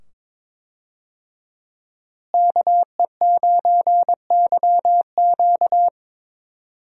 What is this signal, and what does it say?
Morse code, heard as a keyed tone, about 22 wpm, 700 Hz. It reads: KE9YQ
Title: USA Call Signs - 5 Characters - Repeated 2x - 22wpm